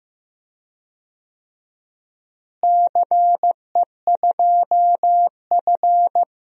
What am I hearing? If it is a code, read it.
Morse code, heard as a keyed tone, about 15 wpm, 700 Hz. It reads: CE2F
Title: DXCC-100 Call Signs - 4 Characters - Repeated 3x - 15wpm